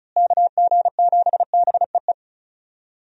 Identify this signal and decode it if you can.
Morse code, heard as a keyed tone, about 34 wpm, 700 Hz. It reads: KG7BEE
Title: USA Call Signs - 6 Characters - Repeated 3x - 35wpm